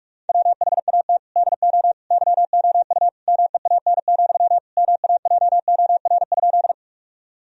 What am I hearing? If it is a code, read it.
Morse code, heard as a keyed tone, about 44 wpm, 700 Hz. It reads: WHAT DO YOU MEAN, MAJOR?
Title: Sentences from Top 1000 Words - Review Part 4 - 45wpm